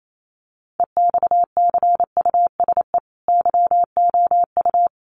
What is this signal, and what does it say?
Morse code, heard as a keyed tone, about 28 wpm, 700 Hz. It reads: EXCUSE YOU
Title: Sentences from Top 700 Words 28wpm